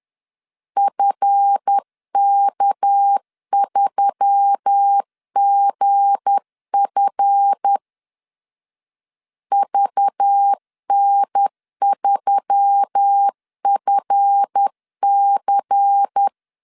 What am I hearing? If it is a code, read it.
Morse code, heard as a keyed tone, about 10 wpm, 800 Hz. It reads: FK3GF VN3FC
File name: LEZIONI / 7 GF9Z / 7 GF 10wpm 2514